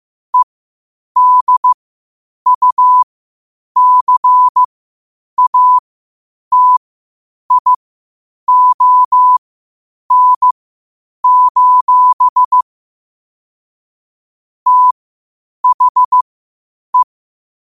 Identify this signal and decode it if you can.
Morse code, heard as a keyed tone, about 15 wpm, spaced as for 8 wpm, 1000 Hz. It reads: EDUCATION: THE
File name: qotdfiles / 8wpm